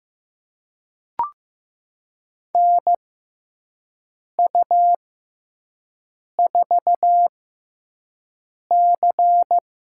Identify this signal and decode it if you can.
Morse code, heard as a keyed tone, about 15 wpm, spaced as for 5 wpm, 700 Hz. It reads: NU4C
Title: USA Call Signs - 4 Characters - 6x Character Spacing - 15wpm